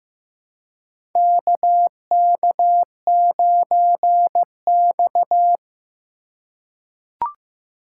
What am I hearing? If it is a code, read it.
Morse code, heard as a keyed tone, about 15 wpm, 700 Hz. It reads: KK9X
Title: USA Call Signs - 4 Characters - 3x Character Spacing - 15wpm